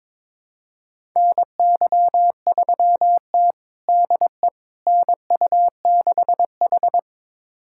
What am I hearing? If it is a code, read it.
Morse code, heard as a keyed tone, about 22 wpm, 700 Hz. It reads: NY3T DE NU6H
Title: USA Callsign DE Callsign - 4 Characters - 22wpm